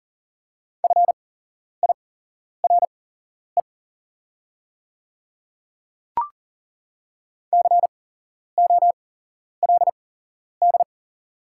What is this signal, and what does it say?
Morse code, heard as a keyed tone, about 40 wpm, spaced as for 11 wpm, 700 Hz. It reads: FIRE COLD
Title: 8x Character Spacing - Top 500 Words - 40wpm